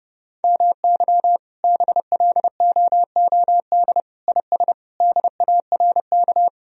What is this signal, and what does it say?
Morse code, heard as a keyed tone, about 30 wpm, 700 Hz. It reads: MY BLOOD IS DARK
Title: Sentences from Top 1000 Words - Review Part 2 - 30wpm